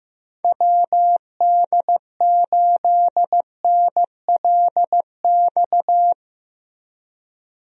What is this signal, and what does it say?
Morse code, heard as a keyed tone, about 16 wpm, 700 Hz. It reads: WD8NLX
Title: USA Call Signs - 6 Characters - Repeated 2x - 15wpm